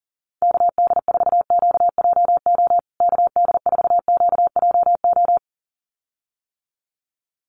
Morse code, heard as a keyed tone, about 40 wpm, 700 Hz. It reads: KD4QJO KD4QJO
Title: USA Call Signs Repeated 2x 40wpm